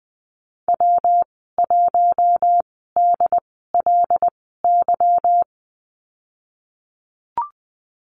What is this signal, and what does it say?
Morse code, heard as a keyed tone, about 20 wpm, spaced as for 14 wpm, 700 Hz. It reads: W1DLY
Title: 154 Lesson 25g - Letter p - Call Signs 20wpm